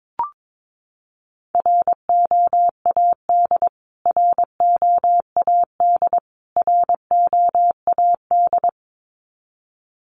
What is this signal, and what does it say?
Morse code, heard as a keyed tone, about 22 wpm, 700 Hz. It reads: ROAD ROAD ROAD
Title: Top 500 Words Repeated 22wpm